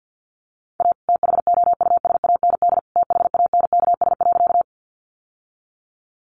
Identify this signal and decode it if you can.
Morse code, heard as a keyed tone, about 48 wpm, 700 Hz. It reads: A THOUSAND THANKS.
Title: Sentences from Top 600 Words 50wpm